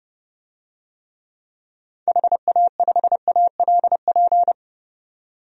Morse code, heard as a keyed tone, about 30 wpm, 700 Hz. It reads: HA5ALP
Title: DXCC-100 Call Signs - 6 Characters - 30wpm